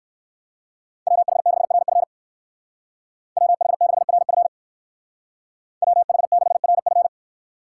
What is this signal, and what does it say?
Morse code, heard as a keyed tone, about 56 wpm, 700 Hz. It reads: WH6RF WH6RF WH6RF
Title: Speed Racing - USA 2x2 Call Signs 45wpm